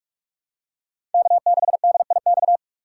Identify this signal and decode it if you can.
Morse code, heard as a keyed tone, about 44 wpm, 690 Hz. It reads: K6DIX
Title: USA Call Signs - 5 Characters - 2x Character Spacing - 45wpm